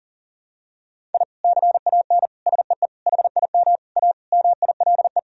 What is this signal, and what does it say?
Morse code, heard as a keyed tone, about 37 wpm, 700 Hz. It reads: I CAN SEE HIM A MILE
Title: Sentences from Top 1000 Words - Review Part 1 - 40wpm